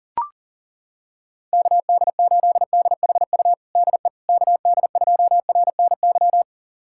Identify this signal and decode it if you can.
Morse code, heard as a keyed tone, about 40 wpm, 700 Hz. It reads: KD8DHU DE KD2RNY
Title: USA Callsign DE Callsign 40wpm